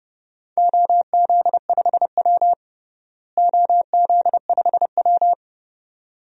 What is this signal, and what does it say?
Morse code, heard as a keyed tone, about 30 wpm, 700 Hz. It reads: OZ5W OZ5W
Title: DXCC-100 Call Signs - 4 Characters - Repeated 2x - 30wpm